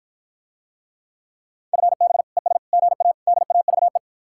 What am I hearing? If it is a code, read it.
Morse code, heard as a keyed tone, about 54 wpm, 700 Hz. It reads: FB ES GA DAVE